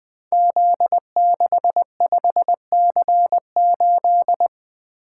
Morse code, heard as a keyed tone, about 21 wpm, 700 Hz. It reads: Z65C8
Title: Code Groups - 5 Letter:Numbers - 20wpm